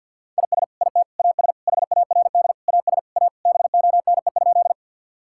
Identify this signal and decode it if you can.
Morse code, heard as a keyed tone, about 50 wpm, 700 Hz. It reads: IS IT AS HARD AS A BONE?
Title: Sentences from Top 1600 Words 50wpm